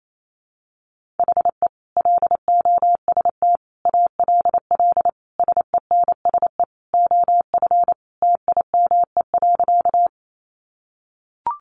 Morse code, heard as a keyed tone, about 28 wpm, 700 Hz. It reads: HE LOST ALL SENSE OF TIME.
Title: Sentences from Top 1400 Words 28wpm